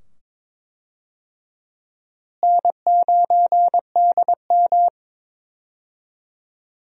Morse code, heard as a keyed tone, about 22 wpm, 700 Hz. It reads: N9DM
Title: USA Call Signs - 4 Characters - Repeated 2x - 22wpm